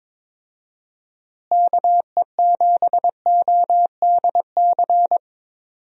Morse code, heard as a keyed tone, about 22 wpm, 700 Hz. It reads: KE7ODC
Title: USA Call Signs - 6 Characters - Repeated 3x - 22wpm